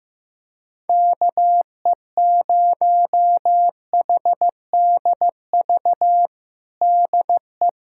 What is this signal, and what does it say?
Morse code, heard as a keyed tone, about 15 wpm, 700 Hz. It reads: KE0HDV DE
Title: USA Callsign DE Callsign 15wpm